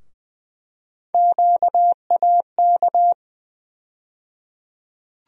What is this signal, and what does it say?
Morse code, heard as a keyed tone, about 20 wpm, 700 Hz. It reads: QAK
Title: USA Call Sign Postfixes - 3 Letters 20wpm